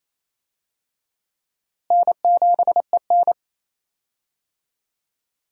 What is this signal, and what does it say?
Morse code, heard as a keyed tone, about 27 wpm, 700 Hz. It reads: N7EN
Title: USA Call Signs - 4 Characters - Repeated 3x - 28wpm